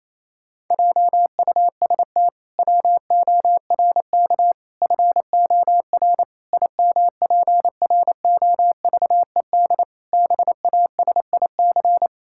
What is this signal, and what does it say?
Morse code, heard as a keyed tone, about 28 wpm, 700 Hz. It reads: JUST WORK FOR IMPROVED BASIC
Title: Pangram Challenge v2 - 28wpm